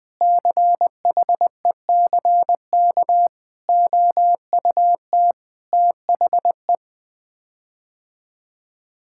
Morse code, heard as a keyed tone, about 21 wpm, 700 Hz. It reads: CHECK OUT THE
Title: Sets of 6 Words - 20wpm